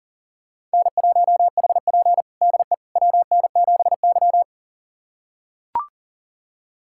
Morse code, heard as a keyed tone, about 40 wpm, 690 Hz. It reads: N1HP DE WN7Y
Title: USA Callsign DE Callsign - 4 Characters - 40wpm